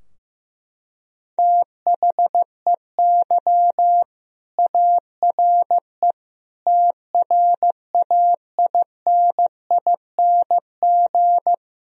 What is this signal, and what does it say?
Morse code, heard as a keyed tone, about 15 wpm, 700 Hz. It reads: THEY ARE TRAINING
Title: Sentences from Top 1000 Words - Review Part 4 - 15wpm